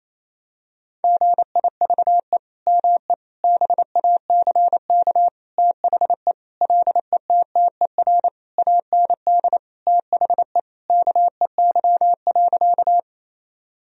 Morse code, heard as a keyed tone, about 28 wpm, 700 Hz. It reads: GIVE ME BACK THE LETTER AND THE KEY.
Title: Sentences from Top 1000 Words - Review Part 4 - 28wpm